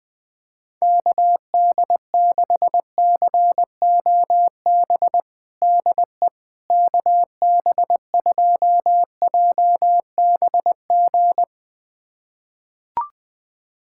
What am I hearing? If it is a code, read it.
Morse code, heard as a keyed tone, about 21 wpm, 700 Hz. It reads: KD6COB DE KB2JBG
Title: USA Callsign DE Callsign - 6 Characters - 20wpm